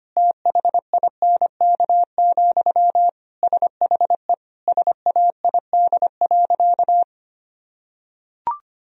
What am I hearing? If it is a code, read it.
Morse code, heard as a keyed tone, about 25 wpm, 700 Hz. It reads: THINK, SHE SAID.
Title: Sentences from Top 1000 Words - Review Part 4 - 25wpm